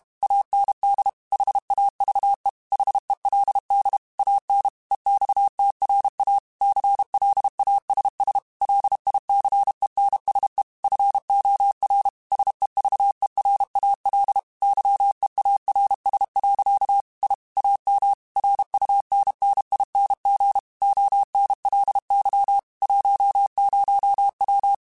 Morse code, heard as a keyed tone, about 32 wpm, 800 Hz. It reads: AND HAVE HELD AN EXTRA CLASS LICENSE FOR SEVERAL YEARS. I AM RUNNING ONLY 10W
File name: QSOs / QSOs-mp3s / WA7QOX 2@30